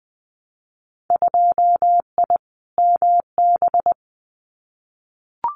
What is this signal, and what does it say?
Morse code, heard as a keyed tone, about 20 wpm, 700 Hz. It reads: 2I MB
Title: Winter Field Day Exchanges Encore 20wpm